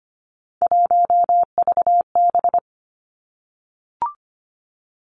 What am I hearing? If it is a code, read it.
Morse code, heard as a keyed tone, about 25 wpm, 700 Hz. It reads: 1VB
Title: Three Challenging Letter-Number Combinations 25wpm